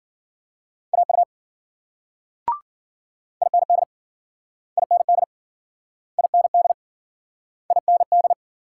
Morse code, heard as a keyed tone, about 53 wpm, 700 Hz. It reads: AU IND IND IND IND